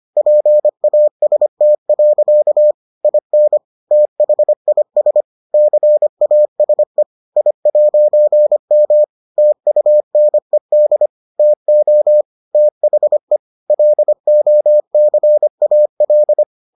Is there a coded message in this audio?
Yes: Morse code from 0.2 to 16.4 s, heard as a keyed tone, about 25 wpm, 600 Hz. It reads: PAST. IN THIS CASE I'M TUNED TO THE LOCAL